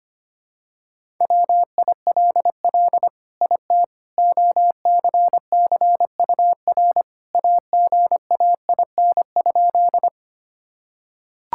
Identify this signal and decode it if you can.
Morse code, heard as a keyed tone, about 25 wpm, 700 Hz. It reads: WILL IT OCCUR AGAIN?
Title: Sentences from Top 1000 Words - Review Part 6 - 25wpm